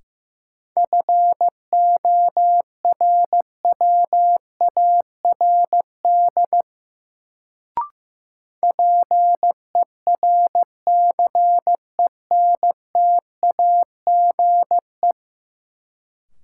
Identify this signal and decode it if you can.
Morse code, heard as a keyed tone, about 15 wpm, 700 Hz. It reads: FORWARD PERCENTAGE
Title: Sentences from Top 1300 Words - New Word Drill - 15wpm